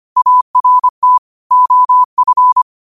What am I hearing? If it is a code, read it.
Morse code, heard as a keyed tone, about 25 wpm, 1000 Hz. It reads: ART OF
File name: 25wpm